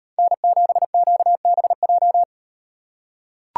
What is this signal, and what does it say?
Morse code, heard as a keyed tone, about 38 wpm, 700 Hz. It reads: N7QBJ